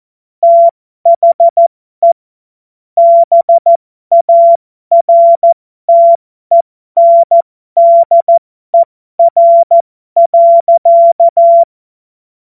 Morse code, heard as a keyed tone, about 14 wpm, spaced as for 12 wpm, 680 Hz. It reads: THE BARTENDER.
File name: Morsekurs / Witze